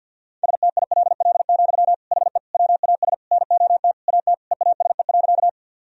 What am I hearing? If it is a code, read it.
Morse code, heard as a keyed tone, about 52 wpm, 700 Hz. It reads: STILL, HE WAS NOT AT EASE.
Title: Sentences from Top 1000 Words - Review Part 1 - 50wpm